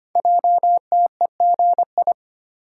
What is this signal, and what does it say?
Morse code, heard as a keyed tone, about 25 wpm, 700 Hz. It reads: JTEGI
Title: Code Groups - 5 Letters - 25wpm